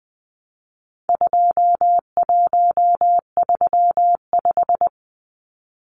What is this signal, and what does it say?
Morse code, heard as a keyed tone, about 20 wpm, 700 Hz. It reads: 2135
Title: Numbers - 4-Digit 20wpm